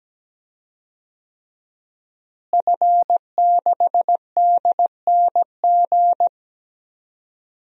Morse code, heard as a keyed tone, about 17 wpm, 700 Hz. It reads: F6DNG
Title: DXCC-100 Call Signs - 5 Characters - Repeated 3x - 17wpm